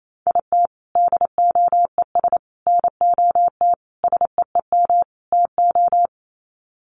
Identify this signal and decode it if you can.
Morse code, heard as a keyed tone, about 28 wpm, 700 Hz. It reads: IT DOES NOT SEEM TO
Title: Sets of 5 Words Encore2 28wpm